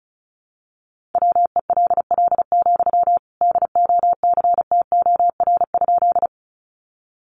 Morse code, heard as a keyed tone, about 34 wpm, 700 Hz. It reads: WELL, DOCTOR?
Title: Sentences from Top 1500 Words 35wpm